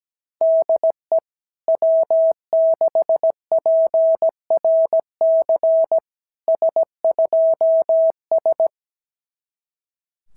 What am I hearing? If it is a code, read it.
Morse code, heard as a keyed tone, about 17 wpm, 650 Hz. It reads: DE W6PRC S2S
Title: SOTA QSO Practice 17wpm